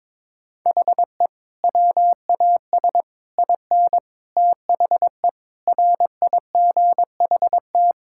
Morse code, heard as a keyed tone, about 22 wpm, 700 Hz. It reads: HE WAS IN THE RIGHT
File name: Sentences with 9-Word Phrases - 22wpm